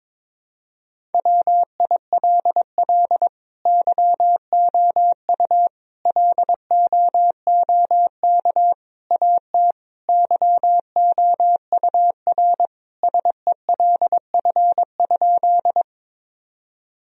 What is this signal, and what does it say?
Morse code, heard as a keyed tone, about 22 wpm, 700 Hz. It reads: WILL YOU LOOK AT YOUR SELF?
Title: Sentences from Top 1000 Words - Review Part 1 - 22wpm